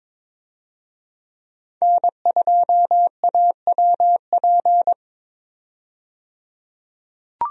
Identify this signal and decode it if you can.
Morse code, heard as a keyed tone, about 22 wpm, 700 Hz. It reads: N2AWP
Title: USA Call Signs - 5 Characters - 22wpm